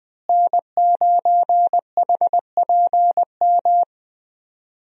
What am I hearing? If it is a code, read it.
Morse code, heard as a keyed tone, about 20 wpm, 700 Hz. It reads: N9HPM
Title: USA Call Signs - 5 Characters - Repeated 3x - 20wpm